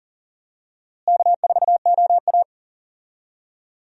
Morse code, heard as a keyed tone, about 39 wpm, 700 Hz. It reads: K4OA
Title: USA Call Signs - 4 Characters - Repeated 2x - 40wpm